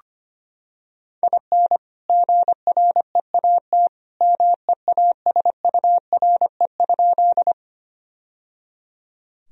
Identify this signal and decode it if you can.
Morse code, heard as a keyed tone, about 25 wpm, 700 Hz. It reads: IN GREAT MEASURE?
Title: Sentences from Top 1000 Words - Review Part 2 - 25wpm